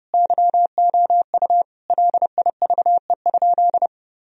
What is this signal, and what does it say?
Morse code, heard as a keyed tone, about 30 wpm, 700 Hz. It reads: YOU LIVE?
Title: Sentences from Top 1000 Words - Review Part 1 - 30wpm